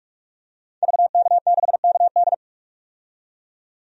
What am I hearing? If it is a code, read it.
Morse code, heard as a keyed tone, about 46 wpm, 700 Hz. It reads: VK6KD
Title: DXCC-100 Call Signs - 5 Characters - 45wpm